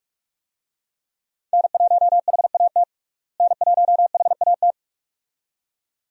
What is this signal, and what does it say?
Morse code, heard as a keyed tone, about 48 wpm, 700 Hz. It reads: N1HAT N1HAT
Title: USA Call Signs - 5 Characters - Repeated 2x - 45wpm